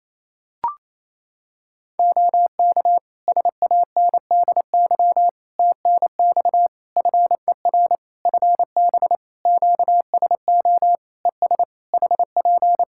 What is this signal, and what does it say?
Morse code, heard as a keyed tone, about 27 wpm, 700 Hz. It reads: OK SANDY TNX FER FB QSO ES HP
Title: K1USN SST QSO Practice 28wpm